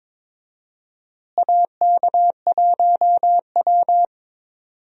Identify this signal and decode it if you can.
Morse code, heard as a keyed tone, about 22 wpm, 700 Hz. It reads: AK1W
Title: USA Call Signs - 4 Characters - Repeated 3x - 22wpm